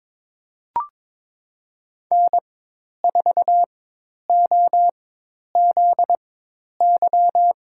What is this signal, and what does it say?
Morse code, heard as a keyed tone, about 23 wpm, spaced as for 10 wpm, 700 Hz. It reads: N4OZY
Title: USA Call Signs - 5 Characters - 4x Character Spacing - 22wpm